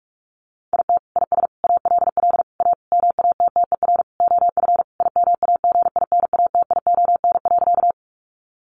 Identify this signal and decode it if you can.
Morse code, heard as a keyed tone, about 44 wpm, 700 Hz. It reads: IT IS ALL A MATTER OF IMAGINATION.
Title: Sentences from Top 1300 Words 45wpm